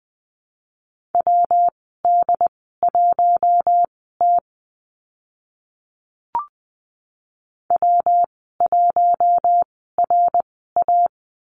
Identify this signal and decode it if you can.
Morse code, heard as a keyed tone, about 20 wpm, spaced as for 14 wpm, 700 Hz. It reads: WD1T W1RA
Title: 140 Lesson 23g - Letter f - Call Signs 20wpm